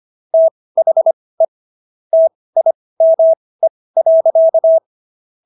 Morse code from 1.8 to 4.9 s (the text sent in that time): TIME.